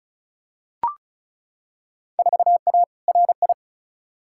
Morse code, heard as a keyed tone, about 36 wpm, 700 Hz. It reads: 4A RI
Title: Field Day Exchanges Encore - 35wpm